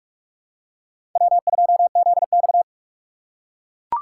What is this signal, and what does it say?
Morse code, heard as a keyed tone, about 42 wpm, 700 Hz. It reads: W2ZX
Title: USA Call Signs - 4 Characters - 6x Character Spacing - 45wpm